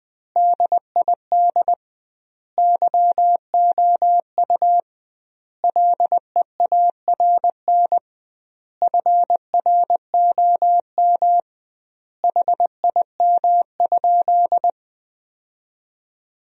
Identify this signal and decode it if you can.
Morse code, heard as a keyed tone, about 21 wpm, 700 Hz. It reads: DID YOU LEARN FROM HIM?
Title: 2x Word-Spacing - Sentences from Top 300 Words - 20wpm